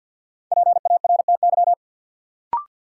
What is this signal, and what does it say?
Morse code, heard as a keyed tone, about 53 wpm, 700 Hz. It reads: PARTY
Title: Sentences from Top 1400 Words - New Word Drill - 50wpm